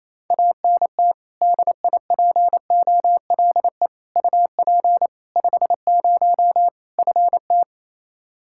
Morse code, heard as a keyed tone, about 28 wpm, 700 Hz. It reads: ANT DIPOLE UP 50 FT
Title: Practice QSO - Exchange 2 - Info - 28wpm